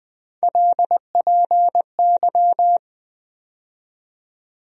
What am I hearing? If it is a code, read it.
Morse code, heard as a keyed tone, about 20 wpm, 700 Hz. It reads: LPY
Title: USA Call Sign Postfixes - 3 Letters 20wpm